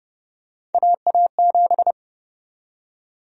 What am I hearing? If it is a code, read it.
Morse code, heard as a keyed tone, about 30 wpm, 700 Hz. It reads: AA7
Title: USA Call Sign Prefixes - 30wpm